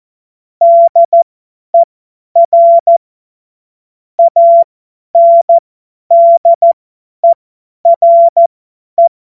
Morse code, heard as a keyed tone, about 14 wpm, spaced as for 10 wpm, 680 Hz. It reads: DER ANDERE
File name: Morsekurs / Witze / Witz 28. Dezember 2021, 50 F70